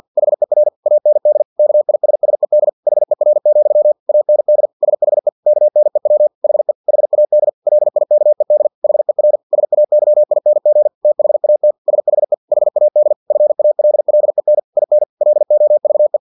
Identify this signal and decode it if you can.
Morse code, heard as a keyed tone, about 50 wpm, 600 Hz. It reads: HER AND KISSED HER, AND SHE KNEW HE HAD LIKED HER SAYING THAT SHE HAD FALLEN IN LOVE
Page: Never 49wpm